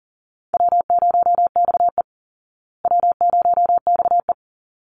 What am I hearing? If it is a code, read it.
Morse code, heard as a keyed tone, about 40 wpm, 700 Hz. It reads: W0XE W0XE
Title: Easy Call Signs Repeated 2x - 40wpm